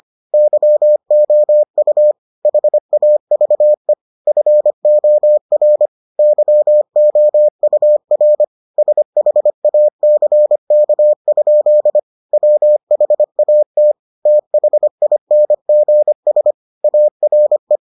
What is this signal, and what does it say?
Morse code, heard as a keyed tone, about 25 wpm, 600 Hz. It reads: YOU HAVE FOR YOUR SHACK? WHAT THINGS ARE